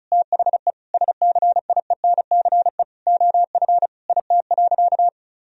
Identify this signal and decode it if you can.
Morse code, heard as a keyed tone, about 34 wpm, 700 Hz. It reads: THE SCIENCE OF IT.
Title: Sentences from Top 1000 Words - Review Part 2 - 35wpm